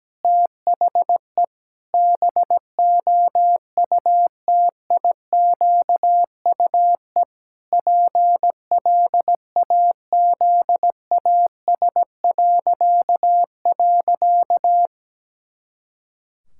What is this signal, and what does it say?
Morse code, heard as a keyed tone, about 17 wpm, 700 Hz. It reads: THE BOUTIQUE PLAZAS..
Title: Pangram Challenge v2 - 17wpm